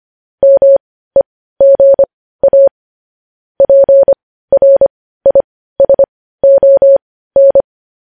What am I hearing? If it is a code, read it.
Morse code, heard as a keyed tone, about 25 wpm, spaced as for 15 wpm, 550 Hz. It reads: MEGA PRISON
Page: news-headlines-in-morse-code-at-15-wpm